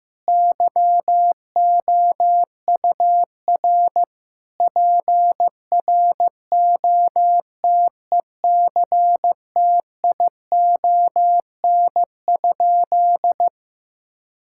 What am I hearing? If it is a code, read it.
Morse code, heard as a keyed tone, about 15 wpm, 700 Hz. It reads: YOUR PROTECTION?
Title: Sentences from Top 1600 Words 15wpm